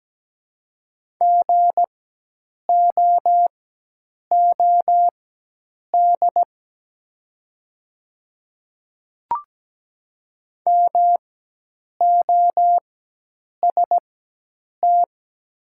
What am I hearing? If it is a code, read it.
Morse code, heard as a keyed tone, about 17 wpm, spaced as for 8 wpm, 700 Hz. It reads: GOOD MOST
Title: 4x Character Spacing - Top 200 Words - 17wpm